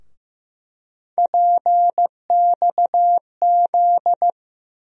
Morse code, heard as a keyed tone, about 15 wpm, 700 Hz. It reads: PXZ
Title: USA Call Sign Postfixes - 3 Letters 15wpm